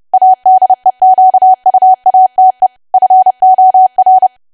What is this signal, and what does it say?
Morse code, heard as a keyed tone, about 30 wpm, 750 Hz. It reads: ADEQUATE FOR
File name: cwa / W1AW-Level3 / 30WPM